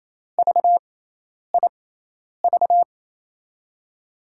Morse code, heard as a keyed tone, about 28 wpm, spaced as for 10 wpm, 700 Hz. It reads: VIV
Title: USA Call Signs - 6 Characters - 6x Character Spacing - 28wpm